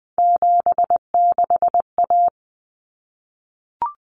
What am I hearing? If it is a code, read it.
Morse code, heard as a keyed tone, about 20 wpm, 700 Hz. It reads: 76A
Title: Common Cut Numbers - 3-Digit 20wpm